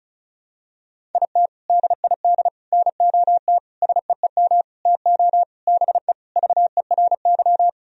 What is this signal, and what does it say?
Morse code, heard as a keyed tone, about 36 wpm, 700 Hz. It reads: IT DID NOT SEEM TO BE VERY
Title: Sentences from Top 1000 Words - Review Part 1 - 35wpm